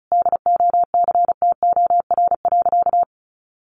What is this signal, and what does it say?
Morse code, heard as a keyed tone, about 34 wpm, 700 Hz. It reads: DOCTOR.
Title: Sentences from Top 1500 Words 35wpm